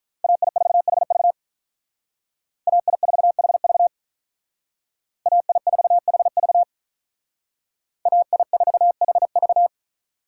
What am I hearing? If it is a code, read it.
Morse code, heard as a keyed tone, about 42 wpm, 700 Hz. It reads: AI4HV AI4HV AI4HV AI4HV
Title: Speed Racing - USA 2x2 Call Signs 35wpm